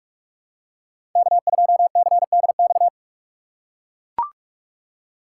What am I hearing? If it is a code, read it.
Morse code, heard as a keyed tone, about 41 wpm, 700 Hz. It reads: K2CDX
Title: K1USN SST QSO Practice - 45wpm